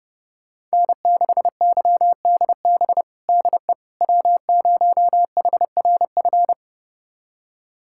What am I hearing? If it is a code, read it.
Morse code, heard as a keyed tone, about 30 wpm, 700 Hz. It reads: N6YDB DE W0HRF
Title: USA Callsign DE Callsign - 5 Characters - 30wpm